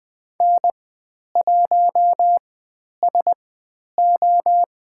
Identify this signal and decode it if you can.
Morse code, heard as a keyed tone, about 20 wpm, spaced as for 10 wpm, 700 Hz. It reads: N1SO
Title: 095 Lesson 16g - Letter c - Call Signs 20wpm